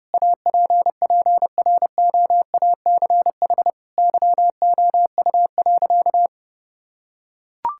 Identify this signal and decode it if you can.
Morse code, heard as a keyed tone, about 30 wpm, 700 Hz. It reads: APPROACH YOU.
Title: Sentences from Top 1000 Words - Review Part 6 - 30wpm